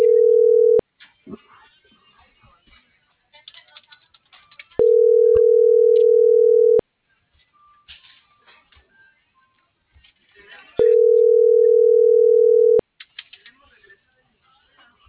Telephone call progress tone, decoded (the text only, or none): ringback tone